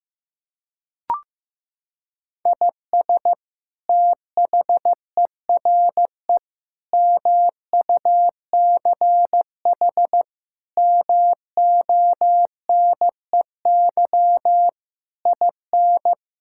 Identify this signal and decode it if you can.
Morse code, heard as a keyed tone, about 16 wpm, 700 Hz. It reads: IS THERE MUCH MONEY IN
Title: Sentences from Top 1000 Words - Review Part 5 - 15wpm